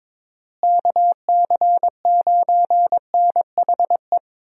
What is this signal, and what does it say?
Morse code, heard as a keyed tone, about 22 wpm, 700 Hz. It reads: KC9NHE
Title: USA Call Signs - 6 Characters - Repeated 3x - 22wpm